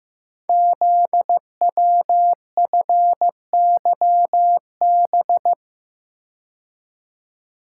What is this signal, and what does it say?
Morse code, heard as a keyed tone, about 15 wpm, 700 Hz. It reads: ZWFYB
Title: Code Groups - 5 Letters - 15wpm